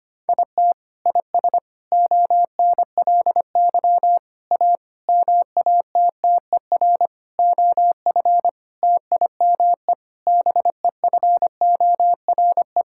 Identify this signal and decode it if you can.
Morse code, heard as a keyed tone, about 25 wpm, 700 Hz. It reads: IT IS ONLY A MATTER OF TIME BEFORE
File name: practice / 25wpm / Sentences with 9-Word Phrases - 25wpm